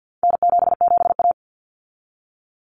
Morse code, heard as a keyed tone, about 50 wpm, 700 Hz. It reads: N77A
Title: Common Cut Numbers - 4-Digit 50wpm